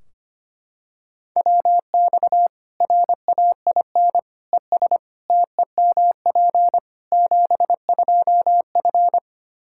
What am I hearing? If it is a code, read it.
Morse code, heard as a keyed tone, about 25 wpm, 700 Hz. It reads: WX RAIN ES TEMP 72F
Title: K1USN SST QSO Practice 25wpm